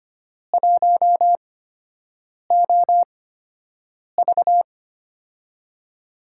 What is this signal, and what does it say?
Morse code, heard as a keyed tone, about 25 wpm, 700 Hz. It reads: 1 O V
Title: USA Call Signs - 8x Character Spacing - 25wpm